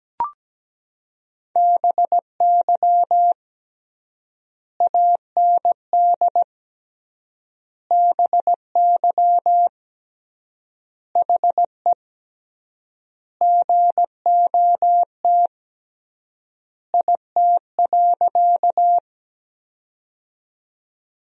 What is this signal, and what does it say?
Morse code, heard as a keyed tone, about 17 wpm, 700 Hz. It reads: BY AND BY HE GOT IT.
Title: 3x Word-Spacing - Sentences from Top 300 Words - 17wpm